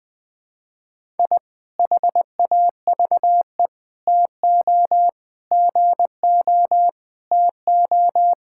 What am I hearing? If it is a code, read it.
Morse code, heard as a keyed tone, about 20 wpm, 700 Hz. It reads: I HAVE TO GO TO
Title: Sentences with 3-Word Phrases - Spoken Beginning - 20wpm